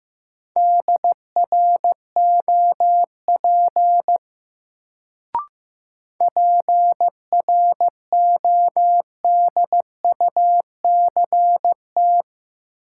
Morse code, heard as a keyed tone, about 15 wpm, 700 Hz. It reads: DROP PRODUCT
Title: Sentences from Top 1400 Words - New Word Drill Alt - 15wpm